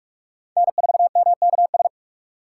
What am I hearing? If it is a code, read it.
Morse code, heard as a keyed tone, about 41 wpm, 700 Hz. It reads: N4MKS